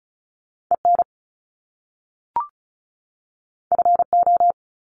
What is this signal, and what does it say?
Morse code, heard as a keyed tone, about 36 wpm, 700 Hz. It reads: EN FO
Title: Top Two Letter Combinations in Words 35wpm